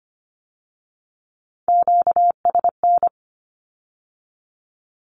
Morse code, heard as a keyed tone, about 25 wpm, 700 Hz. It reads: QSN
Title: Q-Codes with Spoken Meaning - 25wpm